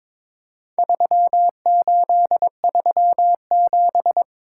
Morse code, heard as a keyed tone, about 22 wpm, 700 Hz. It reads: 3837
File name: practice / 22wpm / Four Letter-Number Combinations - 22wpm